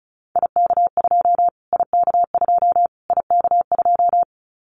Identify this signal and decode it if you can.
Morse code, heard as a keyed tone, about 34 wpm, 700 Hz. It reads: IK2 IK2 IK2
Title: Callsign Prefixes for Top 25 DXCC - Repeated 3x - 35wpm